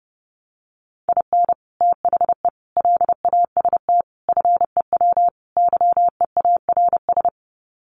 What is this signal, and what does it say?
Morse code, heard as a keyed tone, about 30 wpm, 700 Hz. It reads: IN THE LAST FEW YEARS
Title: Sets of 5 Words 30wpm